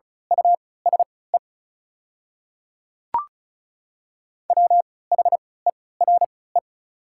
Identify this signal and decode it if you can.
Morse code, heard as a keyed tone, about 36 wpm, spaced as for 20 wpm, 700 Hz. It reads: USE WHERE
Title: 3x Character Spacing - Top 300 Words - 35wpm